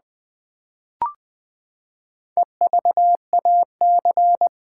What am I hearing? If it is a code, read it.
Morse code, heard as a keyed tone, about 20 wpm, 700 Hz. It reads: EVAC
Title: Build A Word - 4 Letters At A Time - 8-Letter Words - 20wpm